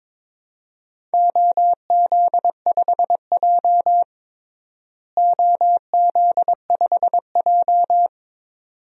Morse code, heard as a keyed tone, about 22 wpm, 700 Hz. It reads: OZ5J OZ5J
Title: DXCC-100 Call Signs - 4 Characters - Repeated 2x - 22wpm